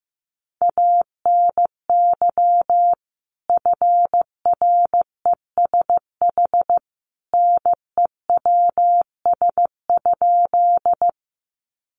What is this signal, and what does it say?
Morse code, heard as a keyed tone, about 15 wpm, 700 Hz. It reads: ANY FRESH NEWS?
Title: Sentences from Top 1200 Words 15wpm